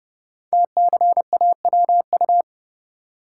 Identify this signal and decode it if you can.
Morse code, heard as a keyed tone, about 30 wpm, 700 Hz. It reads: TCAWU
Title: Code Groups - 5 Letter:Numbers - 30wpm